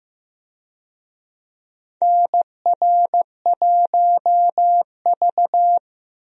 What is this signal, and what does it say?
Morse code, heard as a keyed tone, about 15 wpm, 700 Hz. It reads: NR1V
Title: USA Call Signs - 4 Characters - Repeated 2x - 15wpm